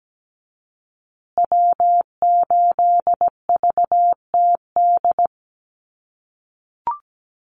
Morse code, heard as a keyed tone, about 17 wpm, 700 Hz. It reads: W8VTD
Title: USA Call Signs Repeated 3x 17wpm